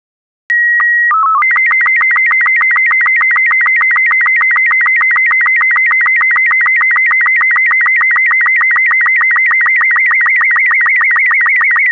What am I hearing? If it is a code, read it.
SSTV (slow-scan television), Robot 36.
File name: robot36-wally